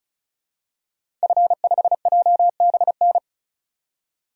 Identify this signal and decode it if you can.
Morse code, heard as a keyed tone, about 34 wpm, 700 Hz. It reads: F5JBN
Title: DXCC-100 Call Signs - 5 Characters - 35wpm